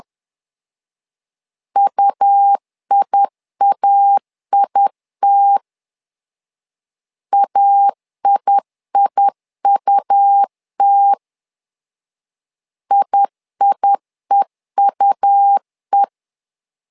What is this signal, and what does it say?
Morse code, heard as a keyed tone, about 10 wpm, 800 Hz. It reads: UIAIT AIIUT IIEUE
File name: LEZIONI / 2 UHDN / 2 U 10wpm 1622